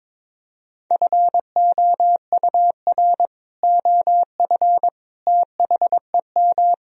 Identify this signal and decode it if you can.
Morse code, heard as a keyed tone, about 22 wpm, 700 Hz. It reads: FOUR OF THEM
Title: Sentences from Top 1000 Words - Review Part 1 - 22wpm